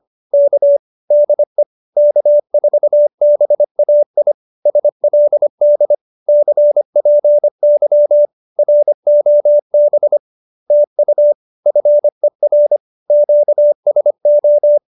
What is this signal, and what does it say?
Morse code, heard as a keyed tone, about 25 wpm, 590 Hz. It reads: K DE K4BAI SLD CPY ROB TU FER QSO